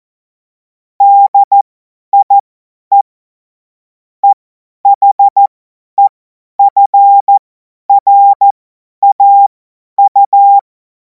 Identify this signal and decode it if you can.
Morse code, heard as a keyed tone, about 14 wpm, spaced as for 10 wpm, 800 Hz. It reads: DIE EHEFRAU
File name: 50 F70